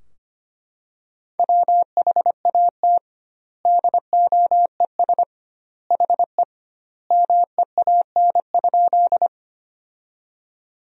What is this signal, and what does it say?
Morse code, heard as a keyed tone, about 25 wpm, 700 Hz. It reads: WHAT DOES HE MEAN?
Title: 2x Word-Spacing - Sentences from Top 200 Words - 25wpm